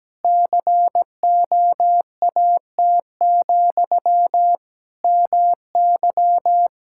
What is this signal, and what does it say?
Morse code, heard as a keyed tone, about 17 wpm, 700 Hz. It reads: COAT, MY
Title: Sentences from Top 1600 Words 17wpm